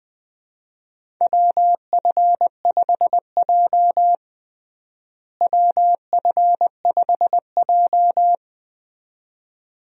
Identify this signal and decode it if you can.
Morse code, heard as a keyed tone, about 20 wpm, 700 Hz. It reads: WF5J WF5J
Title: USA Call Signs - 4 Characters - Repeated 2x - 20wpm